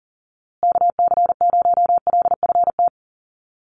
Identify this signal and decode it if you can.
Morse code, heard as a keyed tone, about 38 wpm, 700 Hz. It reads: KC0LFT
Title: USA Call Signs Repeated 2x 40wpm